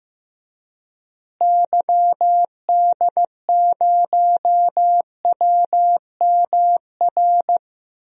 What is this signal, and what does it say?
Morse code, heard as a keyed tone, about 15 wpm, 700 Hz. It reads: YD0WMR
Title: DXCC-100 Call Signs - 6 Characters - Repeated 3x - 15wpm